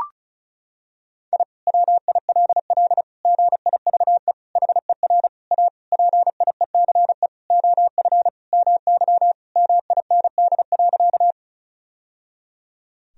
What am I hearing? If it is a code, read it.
Morse code, heard as a keyed tone, about 36 wpm, 700 Hz. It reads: I WILL GIVE HER A PIECE OF MY MIND.
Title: Sentences from Top 1000 Words - Review Part 2 - 35wpm